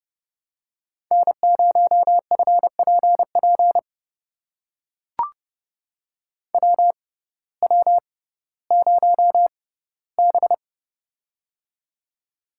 Morse code, heard as a keyed tone, about 30 wpm, spaced as for 17 wpm, 700 Hz. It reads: N0FPP WW0B
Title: USA Call Signs - 6x Character Spacing - 30wpm